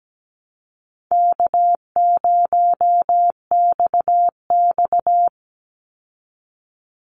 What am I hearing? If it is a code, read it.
Morse code, heard as a keyed tone, about 17 wpm, 700 Hz. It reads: K0XX
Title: Easy Call Signs Repeated 2x - 17wpm